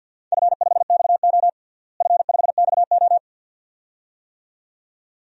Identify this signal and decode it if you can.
Morse code, heard as a keyed tone, about 52 wpm, 700 Hz. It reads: F5XO F5XO
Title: DXCC-100 Call Signs - 4 Characters - Repeated 2x - 50wpm